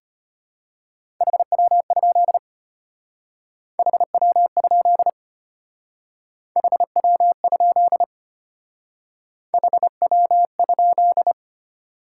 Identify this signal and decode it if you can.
Morse code, heard as a keyed tone, about 31 wpm, 690 Hz. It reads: HW? HW? HW? HW?
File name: Speed-Racing Top 20 QSO Elements - 25wpm